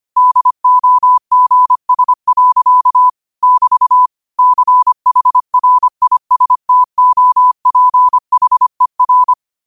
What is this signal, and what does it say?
Morse code, heard as a keyed tone, about 25 wpm, 1000 Hz. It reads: DOGS. <BT> CHRISTOPHER
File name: qotdfiles / 25wpm